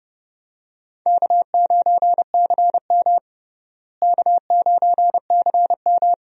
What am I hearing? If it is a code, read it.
Morse code, heard as a keyed tone, about 32 wpm, 700 Hz. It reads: K9CM K9CM
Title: USA Call Signs - 4 Characters - Repeated 2x - 30wpm